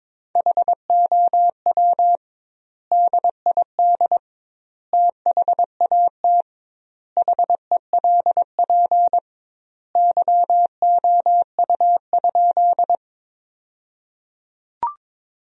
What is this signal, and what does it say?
Morse code, heard as a keyed tone, about 22 wpm, 700 Hz. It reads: HOW DID THAT HELP YOU?
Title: 2x Word-Spacing - Sentences from Top 200 Words - 22wpm